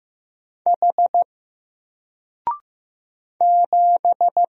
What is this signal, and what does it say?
Morse code, heard as a keyed tone, about 15 wpm, 700 Hz. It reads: H 7
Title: Single Letter-Number 15wpm